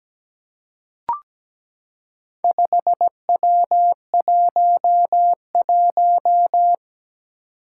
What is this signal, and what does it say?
Morse code, heard as a keyed tone, about 17 wpm, 700 Hz. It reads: 5W11
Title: Four Challenging Letter-Number Combinations 17wpm